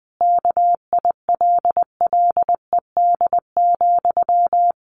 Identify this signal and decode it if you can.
Morse code, heard as a keyed tone, about 20 wpm, 700 Hz. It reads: KILLED,
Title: A Sound Of Thunder 20wpm